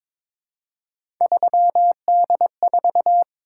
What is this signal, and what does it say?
Morse code, heard as a keyed tone, about 23 wpm, 700 Hz. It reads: 3D4
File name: practice / 22wpm / Three Letter-Number Combinations - 22wpm